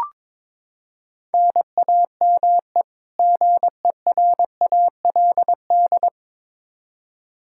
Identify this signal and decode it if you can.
Morse code, heard as a keyed tone, about 21 wpm, 700 Hz. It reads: NAME GERALD
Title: Practice QSO - Exchange 1 - Beginning - 22wpm